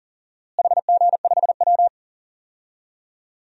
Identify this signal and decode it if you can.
Morse code, heard as a keyed tone, about 40 wpm, 700 Hz. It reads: HG5W
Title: DXCC-100 Call Signs - 4 Characters - Repeated 2x - 40wpm